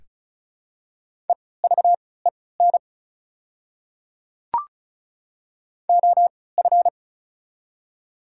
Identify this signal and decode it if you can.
Morse code, heard as a keyed tone, about 36 wpm, spaced as for 20 wpm, 690 Hz. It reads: EVEN OF
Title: 3x Character Spacing - Top 300 Words - 35wpm